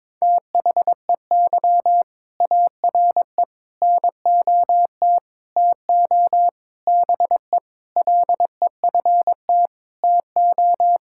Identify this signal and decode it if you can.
Morse code, heard as a keyed tone, about 22 wpm, 700 Hz. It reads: THEY ARE NOT TO BE LEFT TO
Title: Sentences from Top 1000 Words - Review Part 5 - 22wpm